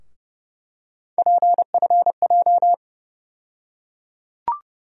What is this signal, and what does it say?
Morse code, heard as a keyed tone, about 29 wpm, 700 Hz. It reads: PFJ
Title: USA Call Sign Postfixes - 3 Letters 30wpm